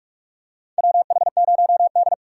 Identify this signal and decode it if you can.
Morse code, heard as a keyed tone, about 46 wpm, 700 Hz. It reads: WH0D
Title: USA Call Signs - 4x Character Spacing - 45wpm